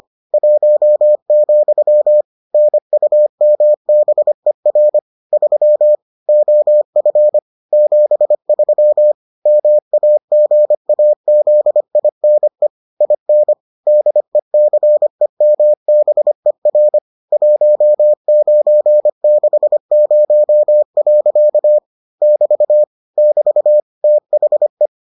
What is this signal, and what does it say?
Morse code, heard as a keyed tone, about 25 wpm, 600 Hz. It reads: 1, NUMBER 3 OF 73 MAGAZINE IN DECEMBER 1960. <BT> <BT> THE